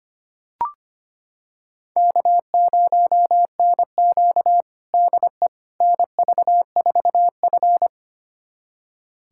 Morse code, heard as a keyed tone, about 25 wpm, 700 Hz. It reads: K0NQ DE NV4F
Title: USA Callsign DE Callsign - 4 Characters - 25wpm